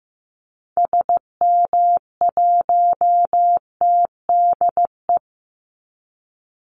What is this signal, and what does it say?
Morse code, heard as a keyed tone, about 15 wpm, 700 Hz. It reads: SM1TDE
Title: Call Signs Repeated 3x - 15wpm